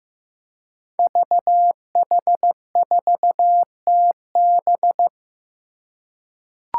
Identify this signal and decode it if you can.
Morse code, heard as a keyed tone, about 15 wpm, 700 Hz. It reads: VH4TB
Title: Five Challenging Letter-Number Combinations 15wpm